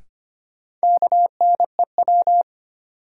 Morse code, heard as a keyed tone, about 25 wpm, 700 Hz. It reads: KNEW
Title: Sentences from Top 400 Words - New Word Drill - 25wpm